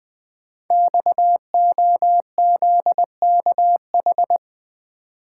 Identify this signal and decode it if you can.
Morse code, heard as a keyed tone, about 20 wpm, 700 Hz. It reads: XOZKH
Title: Code Groups - 5 Letter:Numbers - 20wpm